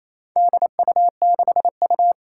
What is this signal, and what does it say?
Morse code, heard as a keyed tone, about 28 wpm, 700 Hz. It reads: DU6U
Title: Four Challenging Letter-Number Combinations 28wpm